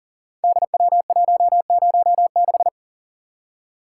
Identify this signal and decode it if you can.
Morse code, heard as a keyed tone, about 39 wpm, 690 Hz. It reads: DW106
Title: Code Groups - 5 Letter:Numbers - 40wpm